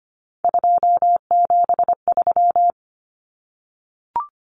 Morse code, heard as a keyed tone, about 25 wpm, 700 Hz. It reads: 273
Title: Common Cut Numbers - 3-Digit 25wpm